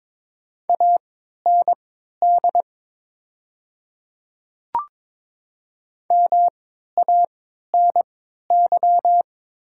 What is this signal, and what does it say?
Morse code, heard as a keyed tone, about 23 wpm, spaced as for 13 wpm, 700 Hz. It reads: AND MANY
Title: 3x Character Spacing - Top 100 Words - 22wpm